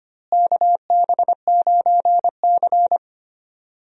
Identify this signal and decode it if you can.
Morse code, heard as a keyed tone, about 25 wpm, 700 Hz. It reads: KB9C
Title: USA Call Signs - 4 Characters - Repeated 3x - 25wpm